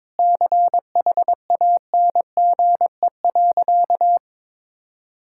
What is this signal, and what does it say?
Morse code, heard as a keyed tone, about 22 wpm, 700 Hz. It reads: CHANGE.
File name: Sentences with 7-Word Phrases - 22wpm